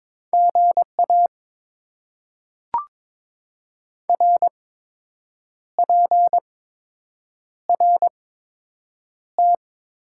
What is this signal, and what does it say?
Morse code, heard as a keyed tone, about 22 wpm, spaced as for 7 wpm, 700 Hz. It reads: GA RPRT
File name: Top 90 QSO Elements - 8x Character Spacing - 22wpm